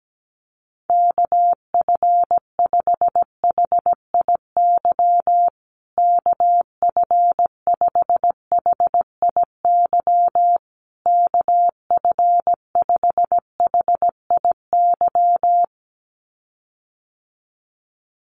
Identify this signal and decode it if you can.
Morse code, heard as a keyed tone, about 17 wpm, 700 Hz. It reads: KF5HIY KF5HIY KF5HIY
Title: USA Call Signs Repeated 3x 17wpm